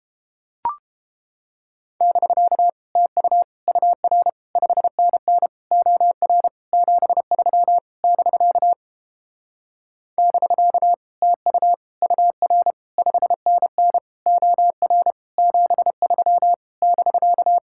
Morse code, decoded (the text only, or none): <BK> TU UR 5NN OR 73 <BK> <BK> TU UR 5NN OR 73 <BK>